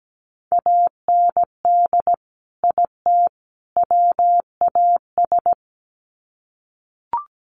Morse code, heard as a keyed tone, about 17 wpm, 700 Hz. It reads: AND IT WAS
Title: Sets of 3 Words 17wpm